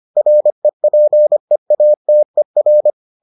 Morse code, heard as a keyed tone, about 25 wpm, 600 Hz. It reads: REPEATER